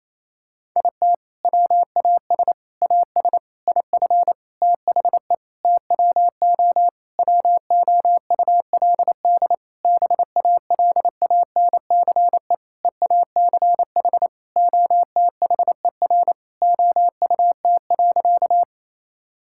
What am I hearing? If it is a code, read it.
Morse code, heard as a keyed tone, about 27 wpm, 700 Hz. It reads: IT WAS AS IF THE TWO WOULD BALANCE EACH OTHER OUT.
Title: Sentences from Top 1000 Words - Review Part 6 - 28wpm